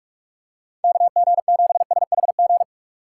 Morse code, heard as a keyed tone, about 46 wpm, 700 Hz. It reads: KG7SHG